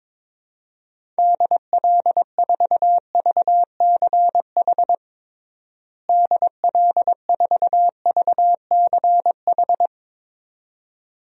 Morse code, heard as a keyed tone, about 22 wpm, 700 Hz. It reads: DL4VCH DL4VCH
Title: DXCC-100 Call Signs - 6 Characters - Repeated 2x - 22wpm